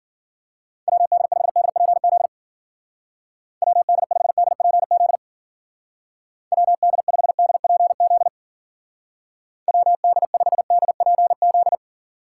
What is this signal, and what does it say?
Morse code, heard as a keyed tone, about 51 wpm, 700 Hz. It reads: WD5DPZ WD5DPZ WD5DPZ WD5DPZ